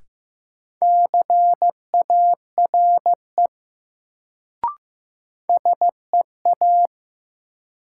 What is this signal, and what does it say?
Morse code, heard as a keyed tone, about 15 wpm, 700 Hz. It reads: CARE SEA
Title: Sentences from Top 300 Words - New Word Drill - 15wpm